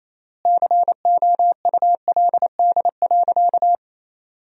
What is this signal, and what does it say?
Morse code, heard as a keyed tone, about 28 wpm, 700 Hz. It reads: COULD.
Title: 3x Word-Spacing - Sentences from Top 500 Words - 28wpm